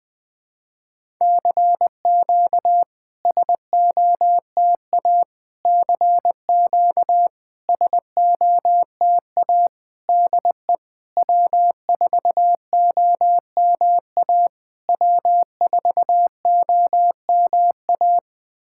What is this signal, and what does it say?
Morse code, heard as a keyed tone, about 20 wpm, 690 Hz. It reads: CQ SOTA CQ SOTA DE W4OMA W4OMA
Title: SOTA QSO Practice 20wpm